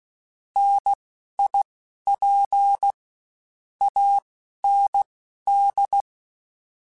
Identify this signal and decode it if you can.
Morse code, heard as a keyed tone, about 16 wpm, spaced as for 11 wpm, 800 Hz. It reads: NIP AND